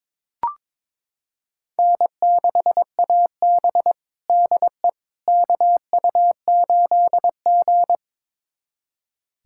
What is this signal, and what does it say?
Morse code, heard as a keyed tone, about 21 wpm, 700 Hz. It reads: N6AB DE KU8G
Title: USA Callsign DE Callsign - 4 Characters - 22wpm